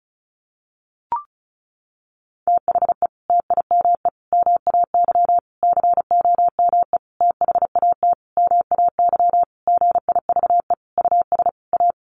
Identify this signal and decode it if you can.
Morse code, heard as a keyed tone, about 34 wpm, 700 Hz. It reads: THE TIME MAY COME THAT MAY GIVE US A
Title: Sentences from Top 1100 Words 35wpm